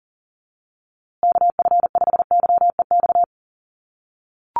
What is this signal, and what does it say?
Morse code, heard as a keyed tone, about 40 wpm, 700 Hz. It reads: KF5YEX
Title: USA Call Signs Repeated 3x 40wpm